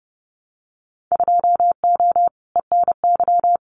Morse code, heard as a keyed tone, about 30 wpm, 700 Hz. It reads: 2O ENY
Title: Winter Field Day Exchanges Encore 30wpm